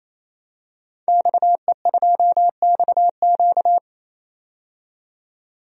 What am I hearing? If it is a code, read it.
Morse code, heard as a keyed tone, about 27 wpm, 700 Hz. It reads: XE2XQ
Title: DXCC-100 Call Signs - 5 Characters - 28wpm